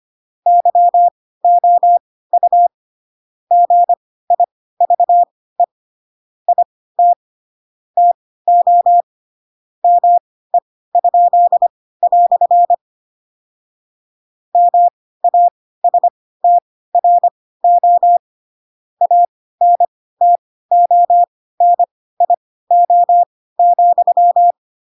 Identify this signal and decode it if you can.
Morse code, heard as a keyed tone, about 25 wpm, spaced as for 16 wpm, 700 Hz. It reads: YOU GIVE IT TO ME?" MASTRO ANTONIO,
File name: Pinocchio / Pinocchio ch2 10 25 ik1hge